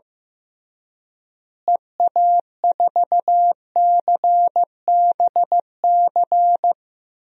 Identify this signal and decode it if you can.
Morse code, heard as a keyed tone, about 15 wpm, 700 Hz. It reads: EA4CBC
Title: DXCC-100 Call Signs - 6 Characters - Repeated 2x - 15wpm